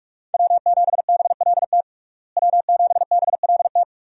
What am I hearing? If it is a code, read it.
Morse code, heard as a keyed tone, about 44 wpm, 700 Hz. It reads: W7BLT W7BLT
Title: USA Call Signs - 5 Characters - Repeated 2x - 45wpm